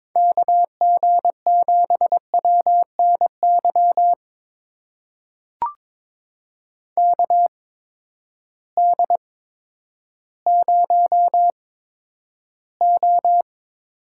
Morse code, heard as a keyed tone, about 22 wpm, spaced as for 11 wpm, 700 Hz. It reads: KG7WNY KD0O
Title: USA Call Signs - 6 Characters - 8x Character Spacing - 22wpm